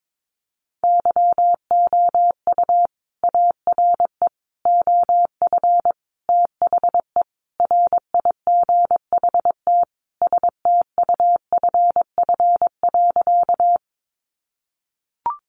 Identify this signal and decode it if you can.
Morse code, heard as a keyed tone, about 22 wpm, 700 Hz. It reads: YOU ARE OF THE RIGHT STUFF.
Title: Sentences from Top 1400 Words 22wpm